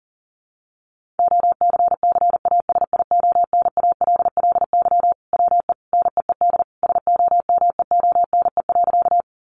Morse code, heard as a keyed tone, about 40 wpm, 700 Hz. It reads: OCCASIONALLY WE NEED SOMEONE.
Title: Sentences from Top 1300 Words 40wpm